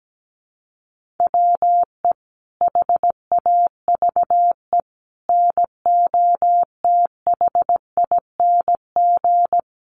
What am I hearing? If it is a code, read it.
Morse code, heard as a keyed tone, about 17 wpm, 700 Hz. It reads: WE HAVE NOTHING
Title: Sentences from Top 1400 Words 17wpm